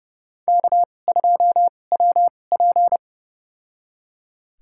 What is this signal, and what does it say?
Morse code, heard as a keyed tone, about 30 wpm, spaced as for 22 wpm, 700 Hz. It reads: K2WP
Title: USA Call Signs - 4 Characters - 2x Character Spacing - 30wpm